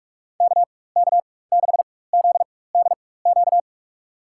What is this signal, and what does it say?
Morse code, heard as a keyed tone, about 46 wpm, spaced as for 21 wpm, 700 Hz. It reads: KK6ZDQ
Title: USA Call Signs - 6 Characters - 4x Character Spacing - 45wpm